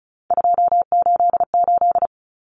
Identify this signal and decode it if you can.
Morse code, heard as a keyed tone, about 34 wpm, 700 Hz. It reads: 288
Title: Common Cut Numbers - 3-Digit 35wpm